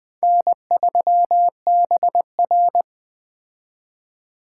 Morse code, heard as a keyed tone, about 21 wpm, 700 Hz. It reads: N3BR
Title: Speed Racing - USA 1x2 Call Signs 17wpm